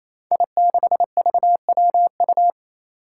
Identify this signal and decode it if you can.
Morse code, heard as a keyed tone, about 27 wpm, 700 Hz. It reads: I6VWU
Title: Five Challenging Letter-Number Combinations 28wpm